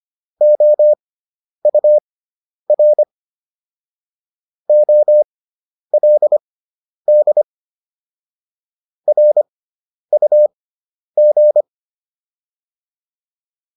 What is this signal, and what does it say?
Morse code, heard as a keyed tone, about 25 wpm, spaced as for 10 wpm, 600 Hz. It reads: OUR OLD RUG